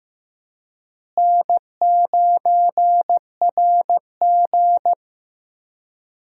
Morse code, heard as a keyed tone, about 15 wpm, 700 Hz. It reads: N9RG
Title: USA Call Signs - 4 Characters - Repeated 3x - 15wpm